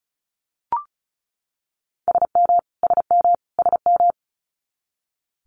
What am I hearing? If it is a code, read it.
Morse code, heard as a keyed tone, about 36 wpm, 700 Hz. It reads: SM SM SM
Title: Multi-letter Phonemes Repeated 3x 35wpm